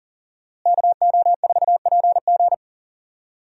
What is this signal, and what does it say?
Morse code, heard as a keyed tone, about 37 wpm, 700 Hz. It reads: KO4PG
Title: USA Call Signs - 5 Characters - Repeated 3x - 40wpm